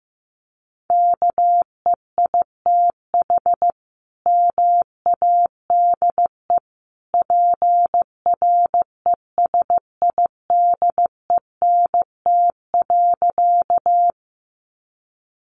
Morse code, heard as a keyed tone, about 15 wpm, 700 Hz. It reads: KEITH MADE PRESIDENT.
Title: A Sound Of Thunder 15wpm